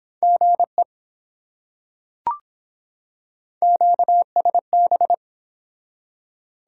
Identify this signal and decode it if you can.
Morse code, heard as a keyed tone, about 24 wpm, 700 Hz. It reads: GE QSB